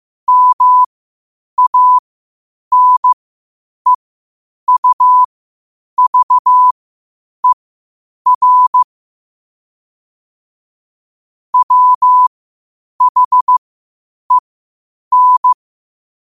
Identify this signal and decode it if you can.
Morse code, heard as a keyed tone, about 15 wpm, spaced as for 8 wpm, 1000 Hz. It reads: MANEUVER WHEN
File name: qotdfiles / 8wpm